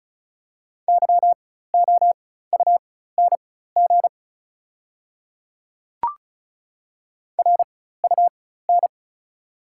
Morse code, heard as a keyed tone, about 35 wpm, spaced as for 16 wpm, 700 Hz. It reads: YOUNG RUN
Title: 4x Character Spacing - Top 500 Words - 35wpm